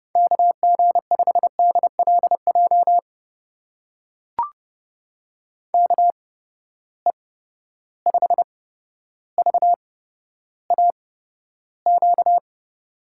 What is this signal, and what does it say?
Morse code, heard as a keyed tone, about 29 wpm, spaced as for 13 wpm, 700 Hz. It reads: KG5DLJ KE5VAQ
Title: USA Call Signs - 6 Characters - 8x Character Spacing - 30wpm